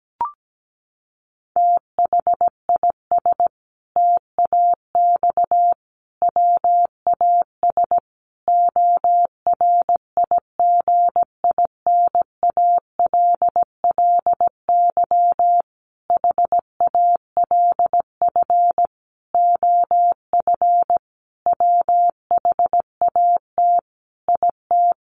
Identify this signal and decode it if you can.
Morse code, heard as a keyed tone, about 17 wpm, 700 Hz. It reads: THIS TAX WAS ORIGINALLY HALF OF WHAT IT
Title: Sentences from Top 1500 Words 17wpm